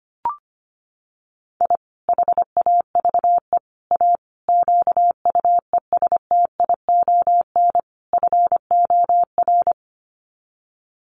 Morse code, heard as a keyed tone, about 25 wpm, 700 Hz. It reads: I HAVE A QUESTION FOR
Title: Sets of 5 Words Encore2 25wpm